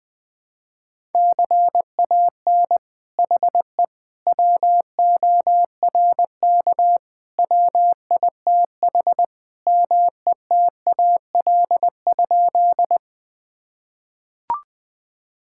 Morse code, heard as a keyed tone, about 21 wpm, 700 Hz. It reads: CAN HE WORK WITH METAL?
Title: Sentences from Top 1000 Words - Review Part 6 - 20wpm